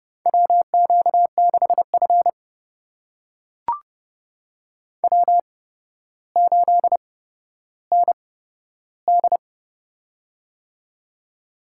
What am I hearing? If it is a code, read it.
Morse code, heard as a keyed tone, about 29 wpm, spaced as for 13 wpm, 700 Hz. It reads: WQ6F W8ND
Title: USA Call Signs - 8x Character Spacing - 30wpm